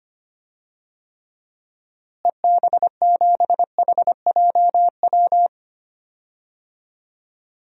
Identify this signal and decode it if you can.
Morse code, heard as a keyed tone, about 25 wpm, 700 Hz. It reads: EB7HJW
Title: DXCC-100 Call Signs - 6 Characters - Repeated 2x - 25wpm